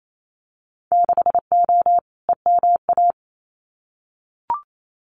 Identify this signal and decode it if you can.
Morse code, heard as a keyed tone, about 28 wpm, 700 Hz. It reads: 6O EMA
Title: Winter Field Day Exchanges Encore 28wpm